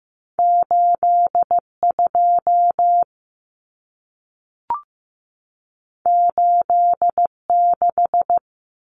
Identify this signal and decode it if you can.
Morse code, heard as a keyed tone, about 15 wpm, 700 Hz. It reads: 82 86
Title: Numbers - 2-Digit 15wpm